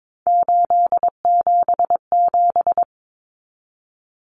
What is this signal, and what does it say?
Morse code, heard as a keyed tone, about 22 wpm, 700 Hz. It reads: 877
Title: Common Cut Numbers - 3-Digit 22wpm